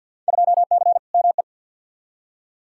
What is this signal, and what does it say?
Morse code, heard as a keyed tone, about 52 wpm, 700 Hz. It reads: 2C ME
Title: Field Day Exchanges Encore - 50wpm